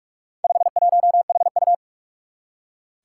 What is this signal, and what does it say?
Morse code, heard as a keyed tone, about 44 wpm, 700 Hz. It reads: 51HU
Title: Four Challenging Letter-Number Combinations 45wpm